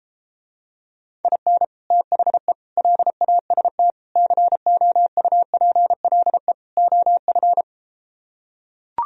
Morse code, heard as a keyed tone, about 33 wpm, 700 Hz. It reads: IN THE LAST COUPLE OF